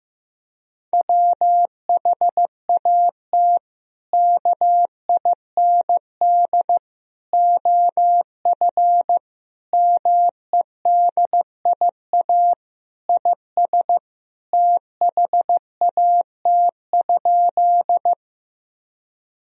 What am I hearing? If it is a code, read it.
Morse code, heard as a keyed tone, about 15 wpm, 700 Hz. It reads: WHAT KIND OF MEDIA IS THAT?
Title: Sentences from Top 1000 Words - Review Part 4 - 15wpm